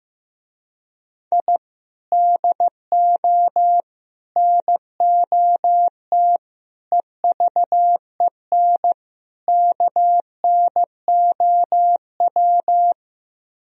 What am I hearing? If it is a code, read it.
Morse code, heard as a keyed tone, about 16 wpm, 700 Hz. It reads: I DO NOT EVEN KNOW
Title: Speed-Racing - Sets of 5 Words 15wpm